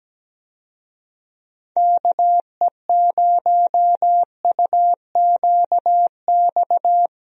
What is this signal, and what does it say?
Morse code, heard as a keyed tone, about 17 wpm, 700 Hz. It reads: KE0UQX
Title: USA Call Signs - 6 Characters - Repeated 3x - 17wpm